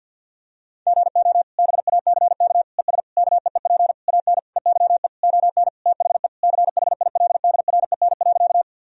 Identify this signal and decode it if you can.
Morse code, heard as a keyed tone, about 50 wpm, 700 Hz. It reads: GO BACK ES KEEP AN EYE ON THE CHILDREN.
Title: Sentences from Top 300 Words and Abbreviations 50wpm